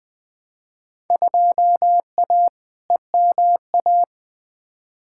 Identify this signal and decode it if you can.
Morse code, heard as a keyed tone, about 20 wpm, 700 Hz. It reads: 2A EMA
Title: Field Day Exchanges - 20wpm